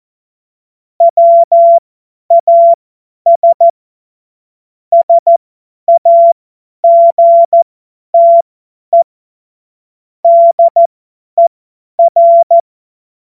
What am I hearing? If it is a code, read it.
Morse code, heard as a keyed tone, about 14 wpm, spaced as for 10 wpm, 680 Hz. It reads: WAS SAGTE DER
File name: Morsekurs / Witze / Witz 01. Februar 2022 50 F70